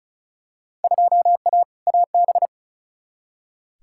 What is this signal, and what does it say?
Morse code, heard as a keyed tone, about 34 wpm, 700 Hz. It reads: 2A AB
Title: Field Day Exchanges - 35wpm